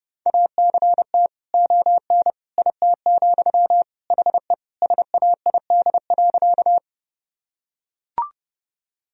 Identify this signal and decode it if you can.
Morse code, heard as a keyed tone, about 32 wpm, 700 Hz. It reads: ACT ON IT, HE SAID.
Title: Sentences from Top 1000 Words - Review Part 1 - 30wpm